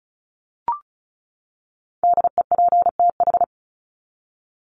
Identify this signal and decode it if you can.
Morse code, heard as a keyed tone, about 34 wpm, 700 Hz. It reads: DEPTH
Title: Top 1001-1250 Words 35wpm